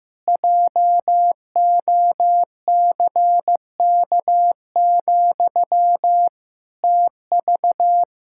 Morse code, heard as a keyed tone, about 15 wpm, 700 Hz. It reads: JOCK, TV
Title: Pangram Challenge v2 - 15wpm